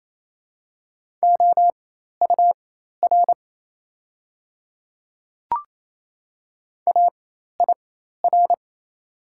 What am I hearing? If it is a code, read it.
Morse code, heard as a keyed tone, about 28 wpm, spaced as for 13 wpm, 700 Hz. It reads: OUR AIR
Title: 4x Character Spacing - Top 200 Words - 28wpm